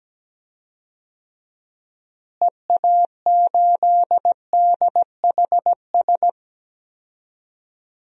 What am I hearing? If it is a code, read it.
Morse code, heard as a keyed tone, about 17 wpm, 700 Hz. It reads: EA8DHS
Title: DXCC-100 Call Signs - 6 Characters - Repeated 2x - 17wpm